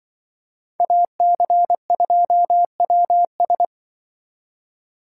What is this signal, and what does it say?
Morse code, heard as a keyed tone, about 24 wpm, 700 Hz. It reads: AC2WS
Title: Speed Racing - USA 2x2 Call Signs 20wpm